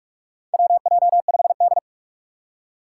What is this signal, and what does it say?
Morse code, heard as a keyed tone, about 46 wpm, 700 Hz. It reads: WJ5D
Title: USA Call Signs - 4 Characters - 3x Character Spacing - 45wpm